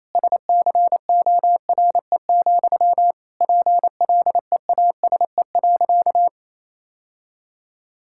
Morse code, heard as a keyed tone, about 28 wpm, 700 Hz. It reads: SCORE, PLEASE.
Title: Sentences from Top 1600 Words 28wpm